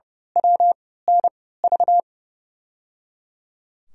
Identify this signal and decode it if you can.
Morse code, heard as a keyed tone, about 31 wpm, spaced as for 17 wpm, 700 Hz. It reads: WNV
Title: USA Call Signs - 3x Character Spacing - 30wpm